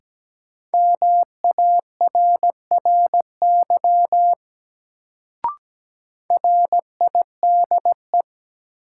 Morse code, heard as a keyed tone, about 17 wpm, 700 Hz. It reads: MARRY RIDE
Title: Sentences from Top 1200 Words - New Word Drill Alt - 17wpm